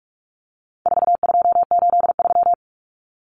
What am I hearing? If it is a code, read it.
Morse code, heard as a keyed tone, about 44 wpm, 700 Hz. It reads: 4283
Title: Common Cut Numbers - 4-Digit 45wpm